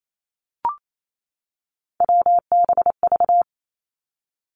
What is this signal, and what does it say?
Morse code, heard as a keyed tone, about 28 wpm, 700 Hz. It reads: WBV
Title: Three Challenging Letter-Number Combinations 28wpm